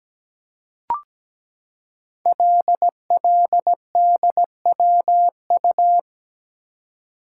Morse code, heard as a keyed tone, about 17 wpm, 700 Hz. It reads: LLDWU
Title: Code Groups - 5 Letter:Numbers - 17wpm